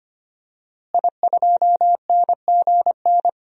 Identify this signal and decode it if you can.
Morse code, heard as a keyed tone, about 25 wpm, 700 Hz. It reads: I2NGN
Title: DXCC-100 Call Signs - 5 Characters - Repeated 2x - 25wpm